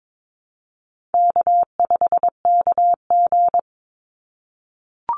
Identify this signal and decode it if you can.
Morse code, heard as a keyed tone, about 22 wpm, 700 Hz. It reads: K5KG
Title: Easy Call Signs - 22wpm